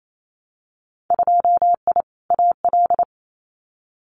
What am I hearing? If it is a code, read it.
Morse code, heard as a keyed tone, about 28 wpm, 700 Hz. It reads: 2I AL
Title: Winter Field Day Exchanges 28wpm